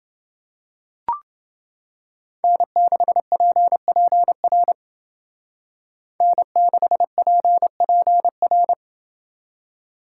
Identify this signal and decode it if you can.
Morse code, heard as a keyed tone, about 29 wpm, 700 Hz. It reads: N6PPR N6PPR